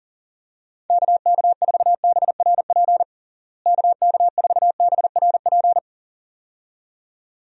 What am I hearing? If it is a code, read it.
Morse code, heard as a keyed tone, about 37 wpm, 700 Hz. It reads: KK4BRP KK4BRP
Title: USA Call Signs - 6 Characters - Repeated 2x - 40wpm